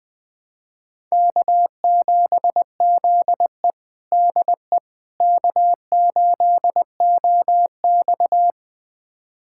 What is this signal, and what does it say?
Morse code, heard as a keyed tone, about 20 wpm, 700 Hz. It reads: K7ZE DE K8OX
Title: USA Callsign DE Callsign - 4 Characters - 20wpm